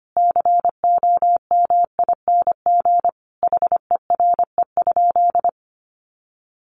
Morse code, heard as a keyed tone, about 25 wpm, 700 Hz. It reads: COMING HERE?
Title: Sentences from Top 1300 Words 25wpm